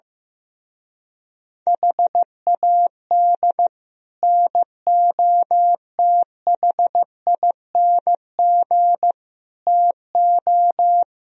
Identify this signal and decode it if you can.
Morse code, heard as a keyed tone, about 15 wpm, 700 Hz. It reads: HAD NOTHING TO
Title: Speed-Racing - Sets of 5 Words 15wpm